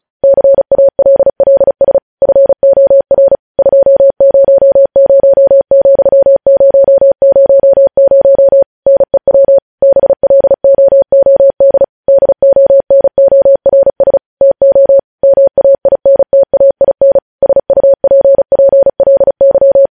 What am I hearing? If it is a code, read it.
Morse code, heard as a keyed tone, about 36 wpm, 550 Hz. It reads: CALLS FOR 200,000 NEW BLOOD DONORS TO MAINTAIN SUPPLY
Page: news headlines in morse code at 35 wpm